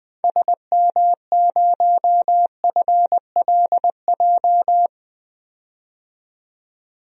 Morse code, heard as a keyed tone, about 21 wpm, 700 Hz. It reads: SM0FLJ
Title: DXCC-100 Call Signs - 6 Characters - 20wpm